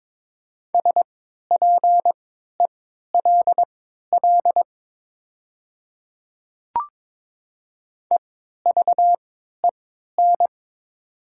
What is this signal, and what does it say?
Morse code, heard as a keyed tone, about 22 wpm, spaced as for 13 wpm, 700 Hz. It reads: SPELL EVEN
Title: 3x Character Spacing - Top 200 Words - 22wpm